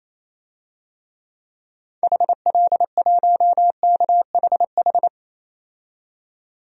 Morse code, heard as a keyed tone, about 28 wpm, 700 Hz. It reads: HL1KHH
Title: DXCC-100 Call Signs - 6 Characters - 28wpm